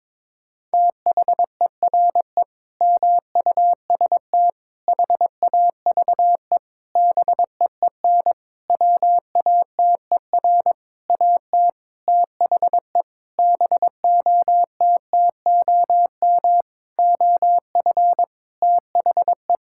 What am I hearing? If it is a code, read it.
Morse code, heard as a keyed tone, about 22 wpm, 700 Hz. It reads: THERE MUST HAVE BEEN WATER AT THE BOTTOM OF THE
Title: Sentences from Top 1000 Words - Review Part 6 - 22wpm